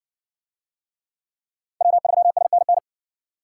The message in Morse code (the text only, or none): W3SNN